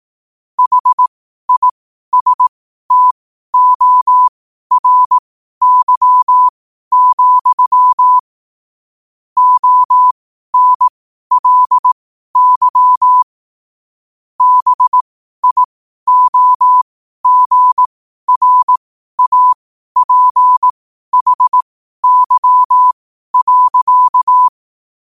Morse code, heard as a keyed tone, about 18 wpm, spaced as for 12 wpm, 1000 Hz. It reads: HISTORY, ONLY BIOGRAPHY.
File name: qotdfiles / 12wpm